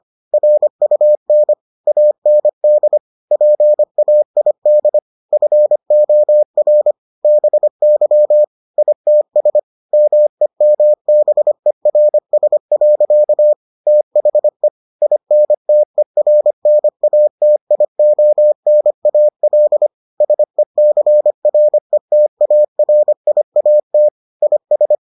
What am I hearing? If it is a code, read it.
Morse code, heard as a keyed tone, about 25 wpm, 600 Hz. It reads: RUN AND PAID FOR BY ITS MEMBERS. THE INTERNATIONAL SECRETARIAT IS